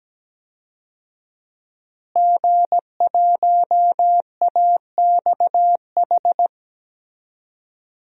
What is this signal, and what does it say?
Morse code, heard as a keyed tone, about 17 wpm, 700 Hz. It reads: G1AXH
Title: DXCC-100 Call Signs - 5 Characters - Repeated 2x - 17wpm